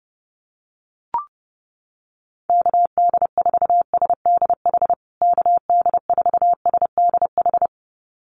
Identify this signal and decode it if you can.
Morse code, heard as a keyed tone, about 30 wpm, 700 Hz. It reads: KD4SDH KD4SDH
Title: USA Call Signs Repeated 2x 30wpm